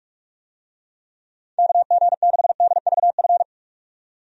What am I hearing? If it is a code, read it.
Morse code, heard as a keyed tone, about 46 wpm, 700 Hz. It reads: KG6DVF
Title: USA Call Signs - 6 Characters - 45wpm